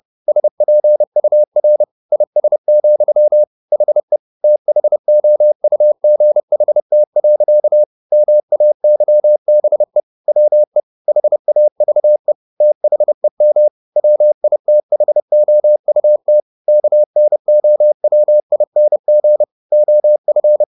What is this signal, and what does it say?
Morse code, heard as a keyed tone, about 30 wpm, 600 Hz. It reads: SPUR IS, HE THOUGHT. MAYBE WE HAVE THEM WITHOUT KNOWING OF